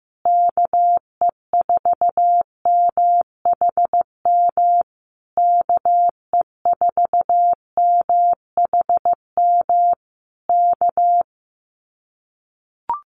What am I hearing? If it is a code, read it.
Morse code, heard as a keyed tone, about 15 wpm, 700 Hz. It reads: KE4MHM KE4MHM K
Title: Common QSO Phrases Encore 15wpm